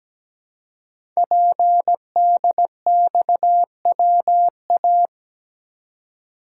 Morse code, heard as a keyed tone, about 17 wpm, 700 Hz. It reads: PDXWA
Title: Code Groups - 5 Letter:Numbers - 17wpm